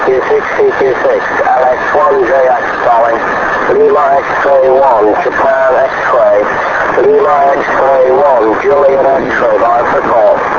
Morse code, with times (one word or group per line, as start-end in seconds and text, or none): none